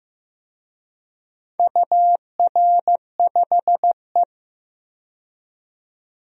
Morse code, heard as a keyed tone, about 16 wpm, 700 Hz. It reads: UR5E
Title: DXCC-100 Call Signs - 4 Characters - 15wpm